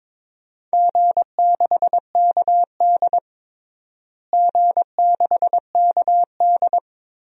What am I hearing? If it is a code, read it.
Morse code, heard as a keyed tone, about 22 wpm, 700 Hz. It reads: G6KD G6KD
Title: DXCC-100 Call Signs - 4 Characters - Repeated 3x - 22wpm